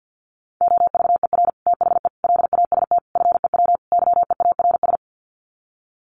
Morse code, heard as a keyed tone, about 50 wpm, 700 Hz. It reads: OVER THE LAST FEW YEARS
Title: Sets of 5 Words Encore 50wpm